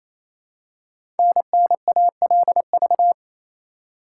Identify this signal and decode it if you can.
Morse code, heard as a keyed tone, about 28 wpm, 700 Hz. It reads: NNALV
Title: Code Groups - 5 Letter:Numbers - 28wpm